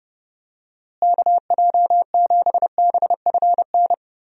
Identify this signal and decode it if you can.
Morse code, heard as a keyed tone, about 30 wpm, 700 Hz. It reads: KJ7BFN
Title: USA Call Signs - 6 Characters - Repeated 3x - 30wpm